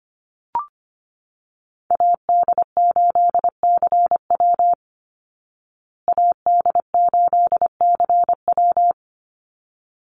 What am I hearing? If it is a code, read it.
Morse code, heard as a keyed tone, about 25 wpm, 700 Hz. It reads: AD8CW AD8CW
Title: Call Signs Repeated 3x - 25wpm